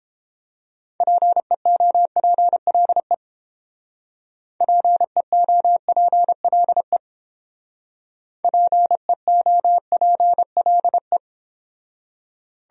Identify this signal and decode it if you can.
Morse code, heard as a keyed tone, about 31 wpm, 700 Hz. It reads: PEOPLE PEOPLE PEOPLE